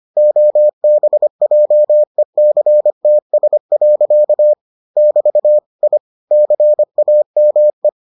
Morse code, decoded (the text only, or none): OBJECTS. <BT> I CAME